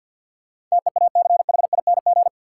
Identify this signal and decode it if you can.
Morse code, heard as a keyed tone, about 53 wpm, 700 Hz. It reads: TEACHING
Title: Sentences from Top 1100 Words - New Word Drill - 50wpm